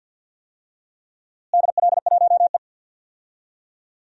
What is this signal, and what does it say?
Morse code, heard as a keyed tone, about 52 wpm, 700 Hz. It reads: DL1E
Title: DXCC-100 Call Signs - 4 Characters - Repeated 2x - 50wpm